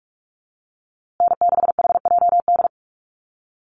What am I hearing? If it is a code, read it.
Morse code, heard as a keyed tone, about 44 wpm, 700 Hz. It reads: N6HJD
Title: USA Call Signs Repeated 3x 45wpm